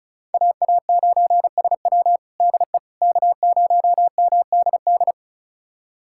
Morse code, heard as a keyed tone, about 36 wpm, 700 Hz. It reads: AA9SW DE K0MDD
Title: USA Callsign DE Callsign 35wpm